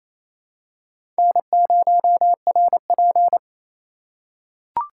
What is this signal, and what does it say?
Morse code, heard as a keyed tone, about 27 wpm, 700 Hz. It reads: N0RP
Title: USA Call Signs - 4 Characters - 4x Character Spacing - 28wpm